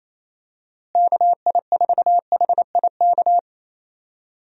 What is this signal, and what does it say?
Morse code, heard as a keyed tone, about 28 wpm, 700 Hz. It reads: KI4HIK
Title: USA Call Signs - 6 Characters - Repeated 3x - 28wpm